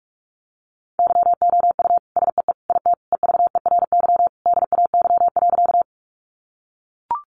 Morse code, heard as a keyed tone, about 44 wpm, 700 Hz. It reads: YOU SEE IT EVERY DAY.
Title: Sentences from Top 200 Words 45wpm